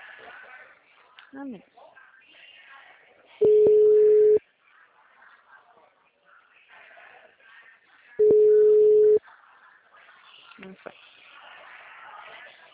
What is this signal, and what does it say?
Ringback tone (Europe, 425 Hz), with speech.